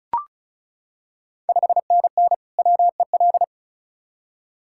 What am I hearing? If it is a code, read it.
Morse code, heard as a keyed tone, about 36 wpm, 700 Hz. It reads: 5NN WEL
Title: Ontario QSO Party Exchange 35wpm